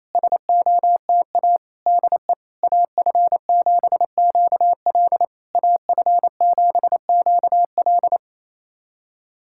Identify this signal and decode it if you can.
Morse code, heard as a keyed tone, about 28 wpm, 690 Hz. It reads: SOTA DE AF7QL AF7QL
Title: SOTA QSO Practice 28wpm